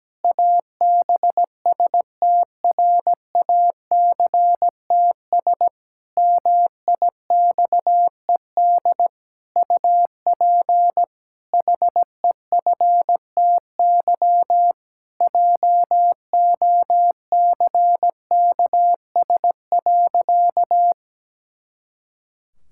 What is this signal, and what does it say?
Morse code, heard as a keyed tone, about 17 wpm, 700 Hz. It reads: ABSTRACTS MIXED UP HEFTY JOCKS.
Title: Pangram Challenge v2 - 17wpm